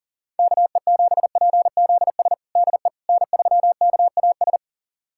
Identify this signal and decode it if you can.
Morse code, heard as a keyed tone, about 40 wpm, 700 Hz. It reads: KE7PZS DE N3KAS
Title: USA Callsign DE Callsign 40wpm